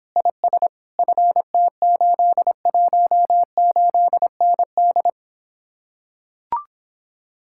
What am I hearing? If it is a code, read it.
Morse code, heard as a keyed tone, about 26 wpm, 700 Hz. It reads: IS FT818ND